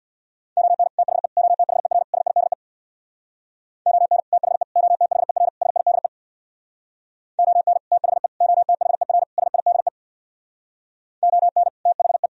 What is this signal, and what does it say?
Morse code, heard as a keyed tone, about 64 wpm, 700 Hz. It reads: ON THE OTHER SIDE ON THE OTHER SIDE ON THE OTHER SIDE ON THE